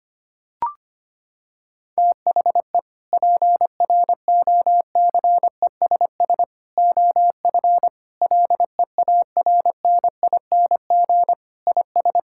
Morse code, heard as a keyed tone, about 25 wpm, 700 Hz. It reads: THE PROCESS OF LEARNING IS
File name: practice / 25wpm / Sentences with 3-Word Phrases - 25wpm